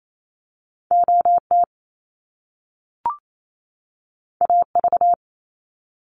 Morse code, heard as a keyed tone, about 28 wpm, 700 Hz. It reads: OT AV
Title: Top Two Letter Combinations in Words 28wpm